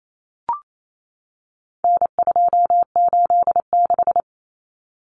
Morse code, heard as a keyed tone, about 28 wpm, 700 Hz. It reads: N286
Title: Common Cut Numbers - 4-Digit 28wpm